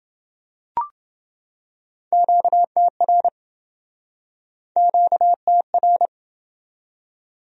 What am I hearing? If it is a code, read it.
Morse code, heard as a keyed tone, about 28 wpm, 700 Hz. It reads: QTR QTR